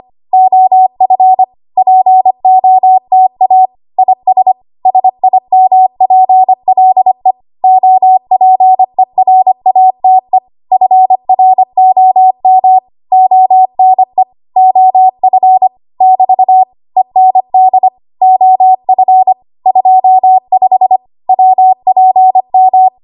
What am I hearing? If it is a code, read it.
Morse code, heard as a keyed tone, about 25 wpm, 750 Hz. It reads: OF POTA IS SIMPLE OPERATE FROM ONE OF <BT> END OF 25 WPM
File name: Archive / 25 WPM / 25WPM